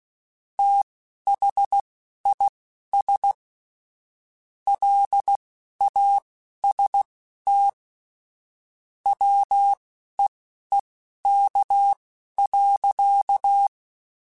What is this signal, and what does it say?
Morse code, heard as a keyed tone, about 16 wpm, spaced as for 11 wpm, 800 Hz. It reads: THIS LAST WEEK.
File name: QSOs / QSOs-mp3s / WD0BGP 2@10